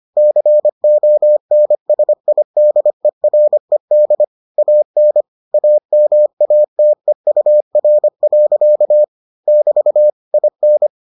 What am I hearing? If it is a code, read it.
Morse code, heard as a keyed tone, about 25 wpm, 600 Hz. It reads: CONSIDERED AN AMATEUR. <BT> IN